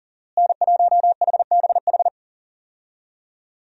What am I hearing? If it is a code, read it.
Morse code, heard as a keyed tone, about 39 wpm, 700 Hz. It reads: N1HBH
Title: USA Call Signs - 5 Characters - Repeated 2x - 40wpm